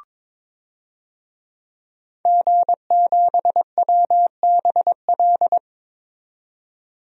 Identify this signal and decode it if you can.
Morse code, heard as a keyed tone, about 22 wpm, 700 Hz. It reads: G7WBL
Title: DXCC-100 Call Signs - 5 Characters - 22wpm